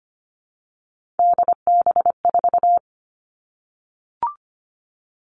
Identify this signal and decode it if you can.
Morse code, heard as a keyed tone, about 25 wpm, 700 Hz. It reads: DB4
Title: All Cut Numbers - 3-Digit 25wpm